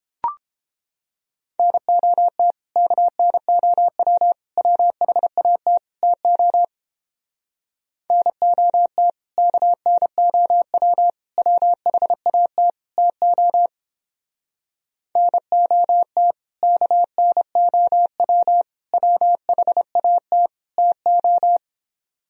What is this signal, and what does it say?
Morse code, heard as a keyed tone, about 29 wpm, 700 Hz. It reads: NOT KNOW WHAT TO NOT KNOW WHAT TO NOT KNOW WHAT TO